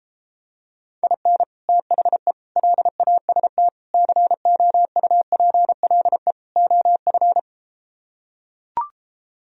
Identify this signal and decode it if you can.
Morse code, heard as a keyed tone, about 33 wpm, 700 Hz. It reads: IN THE LAST COUPLE OF